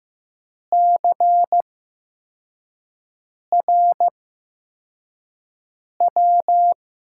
Morse code, heard as a keyed tone, about 15 wpm, spaced as for 4 wpm, 700 Hz. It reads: CRW
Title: USA Call Signs - 6 Characters - 8x Character Spacing - 15wpm